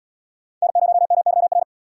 Morse code, heard as a keyed tone, about 74 wpm, 700 Hz. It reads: N0MYR